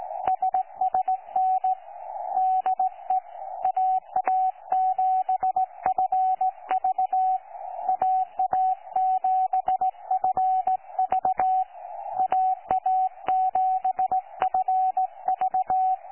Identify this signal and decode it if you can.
Morse code, heard as a keyed tone, about 17 wpm, 750 Hz, with radio noise. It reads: SSN DE AA7FV AA7FV AA7FV
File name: 12 39Z 18088.2kHz utah aa7fv k9tjl